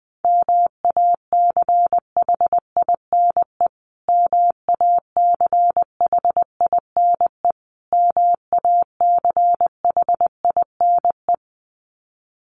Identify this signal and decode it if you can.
Morse code, heard as a keyed tone, about 20 wpm, 700 Hz. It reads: MACHINE MACHINE MACHINE
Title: Top 500 Words Repeated 20wpm